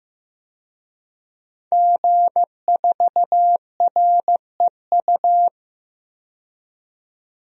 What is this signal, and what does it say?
Morse code, heard as a keyed tone, about 15 wpm, 700 Hz. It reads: G4REU
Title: DXCC-100 Call Signs - 5 Characters - 15wpm